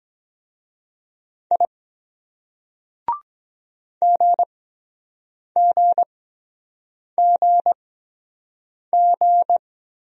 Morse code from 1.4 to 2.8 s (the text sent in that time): I